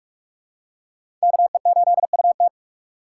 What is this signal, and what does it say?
Morse code, heard as a keyed tone, about 41 wpm, 700 Hz. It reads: KE8UT
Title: USA Call Signs - 4x Character Spacing - 45wpm